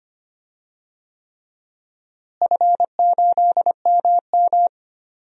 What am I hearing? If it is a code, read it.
Morse code, heard as a keyed tone, about 25 wpm, 700 Hz. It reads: F8MM
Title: DXCC-100 Call Signs - 4 Characters - Repeated 3x - 25wpm